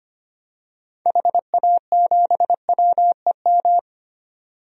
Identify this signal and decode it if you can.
Morse code, heard as a keyed tone, about 25 wpm, 700 Hz. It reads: HA7WEM
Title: DXCC-100 Call Signs - 6 Characters - 25wpm